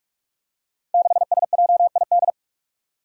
Morse code, heard as a keyed tone, about 44 wpm, 700 Hz. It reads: 6SJID
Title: Five Challenging Letter-Number Combinations 45wpm